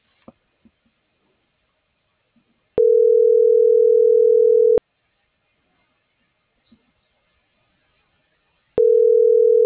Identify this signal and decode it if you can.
Ringback tone (North America), 440 + 480 Hz.